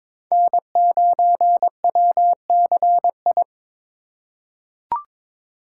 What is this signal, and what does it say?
Morse code, heard as a keyed tone, about 23 wpm, 700 Hz. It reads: N9WCI
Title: USA Call Signs - 2x Character Spacing - 22wpm